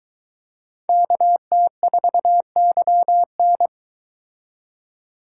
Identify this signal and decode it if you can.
Morse code, heard as a keyed tone, about 23 wpm, 700 Hz. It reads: KT4YN